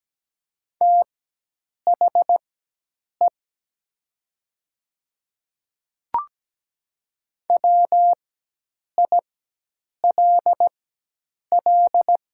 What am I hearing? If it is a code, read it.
Morse code, heard as a keyed tone, about 17 wpm, spaced as for 8 wpm, 700 Hz. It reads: THE WILL
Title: 4x Character Spacing - Top 100 Words - 17wpm